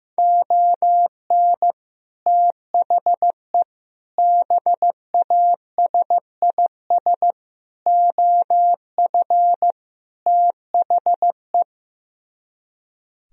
Morse code, heard as a keyed tone, about 16 wpm, 700 Hz. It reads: ON THE BASIS OF THE
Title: Speed-Racing - Sets of 5 Words 15wpm